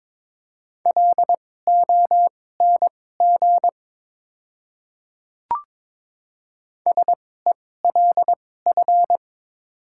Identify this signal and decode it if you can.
Morse code, heard as a keyed tone, about 21 wpm, spaced as for 16 wpm, 700 Hz. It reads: LONG SELF
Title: 2x Character Spacing - Top 200 Words - 22wpm